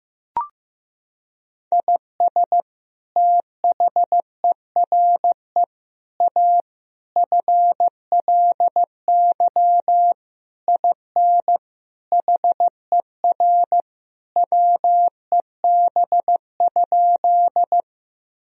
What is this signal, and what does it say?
Morse code, heard as a keyed tone, about 15 wpm, 700 Hz. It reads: IS THERE A FLY IN HER WEB?
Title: Sentences from Top 1000 Words - Review Part 4 - 15wpm